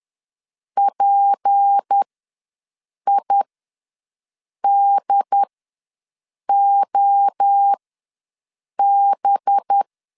Morse code, heard as a keyed tone, about 10 wpm, spaced as for 6 wpm, 800 Hz. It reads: PIDOB
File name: LEZIONI / 6 PKX0 / 6 P ws 10wpm 3751